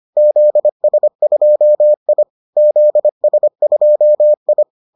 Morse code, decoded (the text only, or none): ZS2I ZS2I